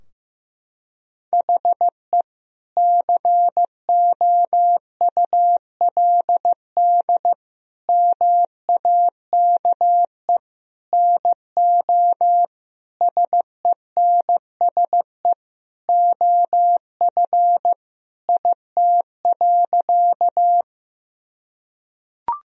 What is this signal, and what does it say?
Morse code, heard as a keyed tone, about 15 wpm, 700 Hz. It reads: HE COULD MAKE NO SENSE OF IT.
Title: Sentences from Top 1000 Words - Review Part 4 - 15wpm